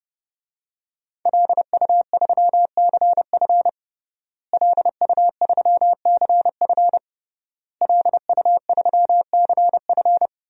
Morse code, heard as a keyed tone, about 30 wpm, 700 Hz. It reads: LU3CF LU3CF LU3CF
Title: DXCC-100 Call Signs - 5 Characters - Repeated 3x - 30wpm